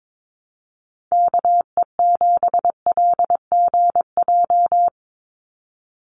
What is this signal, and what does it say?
Morse code, heard as a keyed tone, about 22 wpm, 700 Hz. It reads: KE7LGJ
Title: USA Call Signs 22wpm